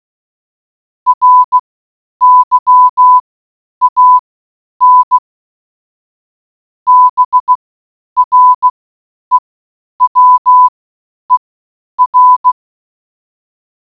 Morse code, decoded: RYAN BREWER